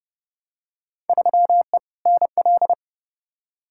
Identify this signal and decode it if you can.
Morse code, heard as a keyed tone, about 30 wpm, 700 Hz. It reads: 3E NL
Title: Field Day Exchanges Encore - 30wpm